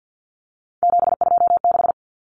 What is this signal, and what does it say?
Morse code, heard as a keyed tone, about 50 wpm, 700 Hz. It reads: 726
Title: Common Cut Numbers - 3-Digit 50wpm